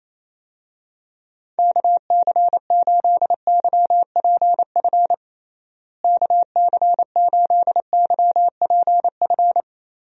Morse code, heard as a keyed tone, about 28 wpm, 700 Hz. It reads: KC8YPF KC8YPF
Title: USA Call Signs - 6 Characters - Repeated 3x - 28wpm